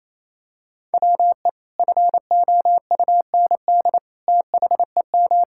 Morse code, heard as a keyed tone, about 28 wpm, 700 Hz. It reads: WE FOUND THEM
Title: Sentences from Top 1000 Words - Review Part 5 - 28wpm